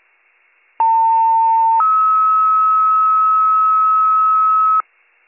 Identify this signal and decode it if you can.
Two-tone page, heard as a one-second tone, then a three-second tone, about 900 then 1290 Hz, with noise. It is heard from 0.8 to 4.8 s.